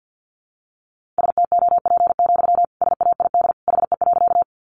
Morse code, heard as a keyed tone, about 50 wpm, 700 Hz. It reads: STOP, SAID HE.
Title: Sentences from Top 300 Words 50wpm